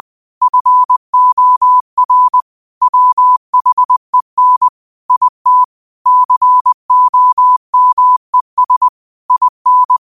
Morse code, heard as a keyed tone, about 20 wpm, 1000 Hz. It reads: FOR WHEN IT COMES IN
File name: qotdfiles / 20wpm